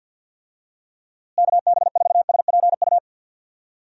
Morse code, heard as a keyed tone, about 52 wpm, 700 Hz. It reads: KB4SPU
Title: USA Call Signs - 6 Characters - Repeated 3x - 50wpm